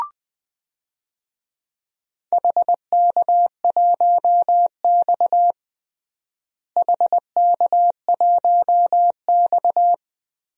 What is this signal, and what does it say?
Morse code, heard as a keyed tone, about 20 wpm, 700 Hz. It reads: HK1X HK1X
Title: DXCC-100 Call Signs - 4 Characters - Repeated 3x - 20wpm